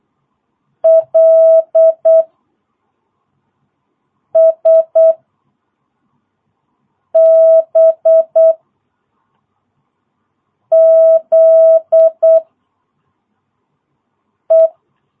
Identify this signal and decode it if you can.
Morse code, heard as a keyed tone, about 8 wpm, 660 Hz. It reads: L S B Z E
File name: jh8cbh / 2022cw